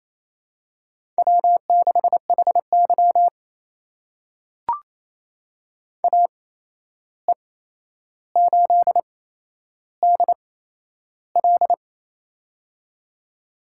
Morse code, heard as a keyed tone, about 28 wpm, spaced as for 11 wpm, 700 Hz. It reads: W6HY AE8DL
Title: USA Call Signs - 8x Character Spacing - 28wpm